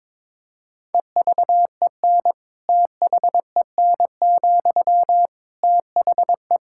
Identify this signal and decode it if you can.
Morse code, heard as a keyed tone, about 22 wpm, 700 Hz. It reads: EVEN THEN, THE
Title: Sentences from Top 1600 Words 22wpm